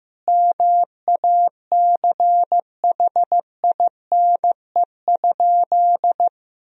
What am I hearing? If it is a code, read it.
Morse code, heard as a keyed tone, about 15 wpm, 700 Hz. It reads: MACHINE?
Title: Sentences from Top 1000 Words - Review Part 2 - 15wpm